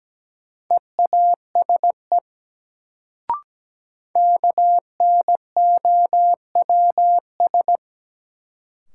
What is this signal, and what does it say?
Morse code, heard as a keyed tone, about 17 wpm, 700 Hz. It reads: EASE KNOWS
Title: Sentences from Top 1000 Words - Review Part 1 - New Word Drill - 17wpm